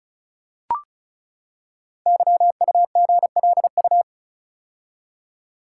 Morse code, heard as a keyed tone, about 36 wpm, 690 Hz. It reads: YUGLU
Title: Code Groups - 5 Letter:Numbers - 35wpm